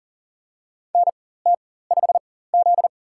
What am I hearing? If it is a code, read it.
Morse code, heard as a keyed tone, about 40 wpm, spaced as for 19 wpm, 700 Hz. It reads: NT5Z
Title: USA Call Signs - 4x Character Spacing - 40wpm